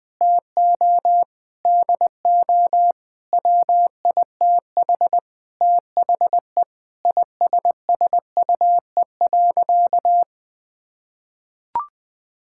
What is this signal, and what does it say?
Morse code, heard as a keyed tone, about 20 wpm, 700 Hz. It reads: TO DO WITH THE ISSUE.
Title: Sentences with 6-Word Phrases - 20wpm